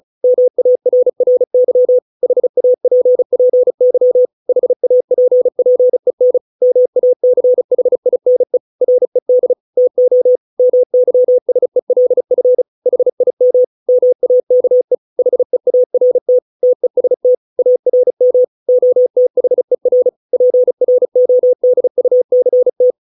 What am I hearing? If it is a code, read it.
Morse code, heard as a keyed tone, about 34 wpm, 500 Hz. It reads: MARRY HAPPY HAPPEN MACHINE RED TO MYSELF HIM MAKE HEART TEST ARM OTHER PRODUCT